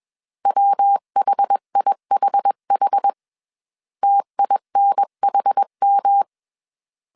Morse code, heard as a keyed tone, about 21 wpm, 800 Hz. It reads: WHIHH TINHM